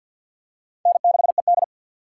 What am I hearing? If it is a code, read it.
Morse code, heard as a keyed tone, about 50 wpm, 700 Hz. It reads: N6ED
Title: USA Call Signs - 4 Characters - 3x Character Spacing - 50wpm